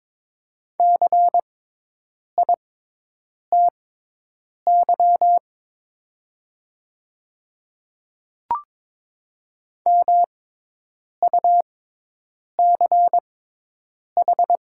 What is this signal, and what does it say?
Morse code, heard as a keyed tone, about 22 wpm, spaced as for 8 wpm, 700 Hz. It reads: CITY MUCH
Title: 6x Character Spacing - Top 300 Words - 22wpm